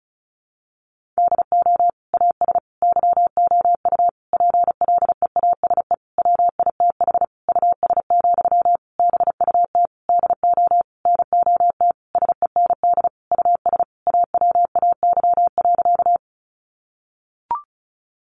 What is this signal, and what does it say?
Morse code, heard as a keyed tone, about 34 wpm, 700 Hz. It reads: DO AS YOU PLEASE WITH US, BUT DO NOT SEND US AWAY.
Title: Sentences from Top 1500 Words 35wpm